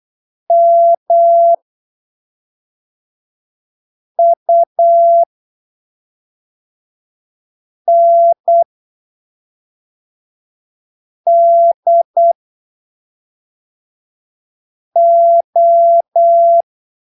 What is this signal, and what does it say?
Morse code, heard as a keyed tone, about 8 wpm, spaced as for 3 wpm, 680 Hz. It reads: MUNDO
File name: arquivos / texto / texto 16 08ppm